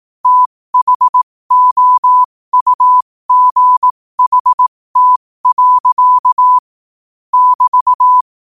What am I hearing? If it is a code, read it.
Morse code, heard as a keyed tone, about 18 wpm, spaced as for 15 wpm, 1000 Hz. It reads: THOUGHT. <BT>
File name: 15wpm